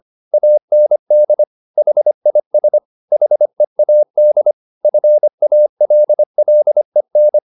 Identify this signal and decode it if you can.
Morse code, heard as a keyed tone, about 25 wpm, 600 Hz. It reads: AND HIS HEAD FALLEN
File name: the old man and the sea part 1 25wpm